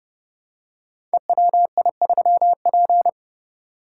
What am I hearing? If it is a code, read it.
Morse code, heard as a keyed tone, about 28 wpm, 700 Hz. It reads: EWI3P
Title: Code Groups - 5 Letter:Numbers - 30wpm